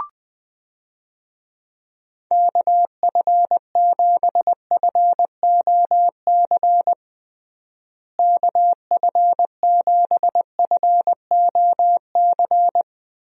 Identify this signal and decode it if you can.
Morse code, heard as a keyed tone, about 20 wpm, 700 Hz. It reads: KF7FOC KF7FOC
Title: USA Call Signs - 6 Characters - Repeated 2x - 20wpm